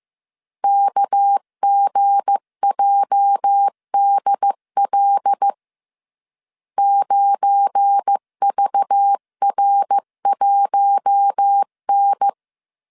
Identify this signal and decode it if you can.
Morse code, heard as a keyed tone, about 15 wpm, 800 Hz. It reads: KGJDL 9VR1N
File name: CW / DET / 1 9 1500C S B 14wpm 3302